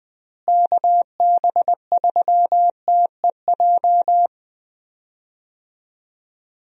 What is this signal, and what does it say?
Morse code, heard as a keyed tone, about 20 wpm, 700 Hz. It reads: KB3TEJ
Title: USA Call Signs - 6 Characters - Repeated 2x - 20wpm